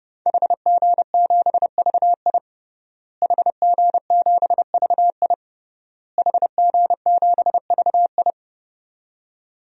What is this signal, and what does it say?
Morse code, heard as a keyed tone, about 30 wpm, 700 Hz. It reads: HG7VI HG7VI HG7VI
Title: DXCC-100 Call Signs - 5 Characters - Repeated 3x - 30wpm